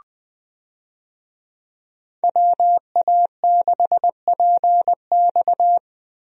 Morse code, heard as a keyed tone, about 21 wpm, 700 Hz. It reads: WA6PX
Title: USA Call Signs - 5 Characters - Repeated 3x - 20wpm